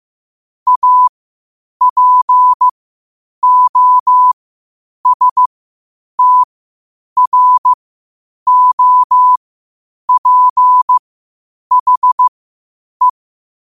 Morse code, heard as a keyed tone, about 15 wpm, spaced as for 8 wpm, 1000 Hz. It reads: APOSTROPHE